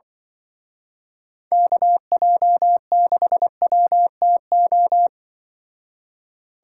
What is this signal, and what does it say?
Morse code, heard as a keyed tone, about 23 wpm, 700 Hz. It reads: KJ6WTO
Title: Speed Racing - USA Call Signs 20wpm